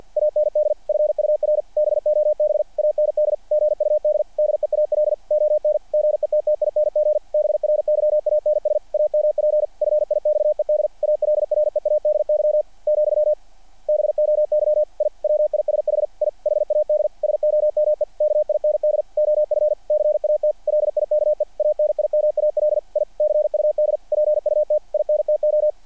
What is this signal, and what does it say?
Morse code, heard as a keyed tone, about 48 wpm, 600 Hz. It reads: AND FUR BOB AND GUD DEAL ON GETTING BRYANS AMP FIXED ALREADY , BOY I WISH I HAD SOME KIND OF CAT LIKE ANIMAL I CUD PUT INTO
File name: archived / CRSnet